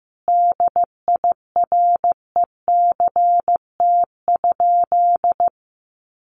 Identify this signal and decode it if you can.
Morse code, heard as a keyed tone, about 15 wpm, 700 Hz. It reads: DIRECT?
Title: Common QSO Phrases 15wpm